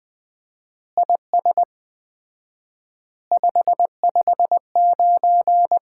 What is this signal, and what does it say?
Morse code, heard as a keyed tone, about 21 wpm, 700 Hz. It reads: IS 559
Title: Common QSO Phrases - 4x Word-Spacing - 20wpm